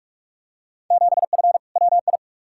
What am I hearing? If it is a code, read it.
Morse code, heard as a keyed tone, about 45 wpm, 700 Hz. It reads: 7F WI
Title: Field Day Exchanges Encore - 45wpm